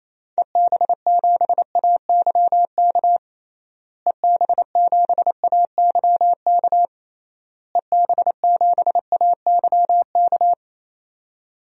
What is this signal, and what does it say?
Morse code, heard as a keyed tone, about 28 wpm, 700 Hz. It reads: EB7AYK EB7AYK EB7AYK
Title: DXCC-100 Call Signs - 6 Characters - Repeated 3x - 28wpm